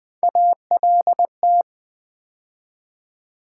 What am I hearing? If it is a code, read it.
Morse code, heard as a keyed tone, about 20 wpm, 700 Hz. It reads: ALT
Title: Build A Word - 3 Letters At A Time - 8-Letter Words - 20wpm